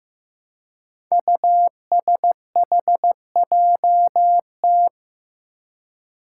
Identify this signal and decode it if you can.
Morse code, heard as a keyed tone, about 15 wpm, 700 Hz. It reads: USHJT
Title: Five Challenging Letter-Number Combinations 15wpm